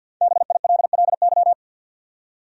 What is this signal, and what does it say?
Morse code, heard as a keyed tone, about 53 wpm, 700 Hz. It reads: BILLY
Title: USA Common Names - 50wpm